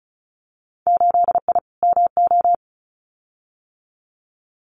Morse code, heard as a keyed tone, about 34 wpm, 700 Hz. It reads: 8I MO
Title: Winter Field Day Exchanges Encore 35wpm